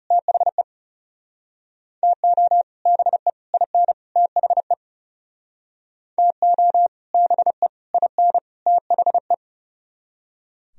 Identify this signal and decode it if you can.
Morse code, heard as a keyed tone, about 38 wpm, 700 Hz. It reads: THE TO BE IN THE TO BE IN THE